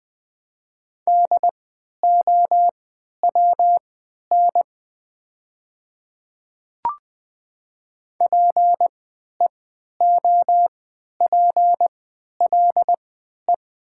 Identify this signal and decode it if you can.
Morse code, heard as a keyed tone, about 20 wpm, spaced as for 11 wpm, 700 Hz. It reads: DOWN PEOPLE
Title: 3x Character Spacing - Top 200 Words - 20wpm